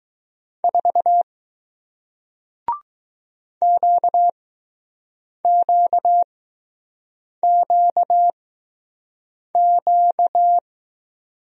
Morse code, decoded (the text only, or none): 4 QQQQ